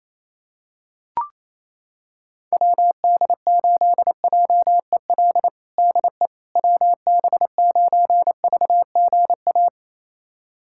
Morse code, heard as a keyed tone, about 28 wpm, 700 Hz. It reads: WD8JEL DE WB9VGA
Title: USA Callsign DE Callsign - 6 Characters - 28wpm